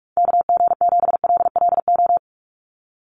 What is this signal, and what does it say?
Morse code, heard as a keyed tone, about 44 wpm, 700 Hz. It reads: KG7LLO
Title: USA Call Signs 45wpm